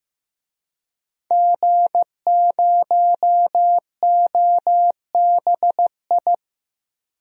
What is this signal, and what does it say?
Morse code, heard as a keyed tone, about 15 wpm, 700 Hz. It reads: G0OBI
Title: DXCC-100 Call Signs - 5 Characters - Repeated 2x - 15wpm